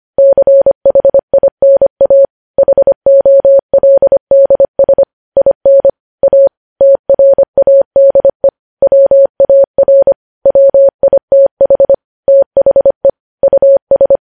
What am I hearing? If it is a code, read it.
Morse code, heard as a keyed tone, about 25 wpm, 550 Hz. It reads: CHINA HOLDS IN A TRADE WAR WITH THE US
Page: News Headlines in Morse Code at 25 WPM